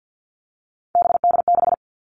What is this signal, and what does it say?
Morse code, heard as a keyed tone, about 50 wpm, 700 Hz. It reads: BD6
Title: All Cut Numbers - 3-Digit 50wpm